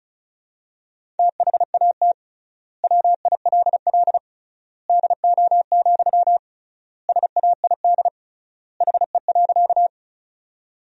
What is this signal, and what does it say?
Morse code, heard as a keyed tone, about 36 wpm, 700 Hz. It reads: THAT WILL DO, SAID HE.
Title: 3x Word-Spacing - Sentences from Top 100 Words - 35wpm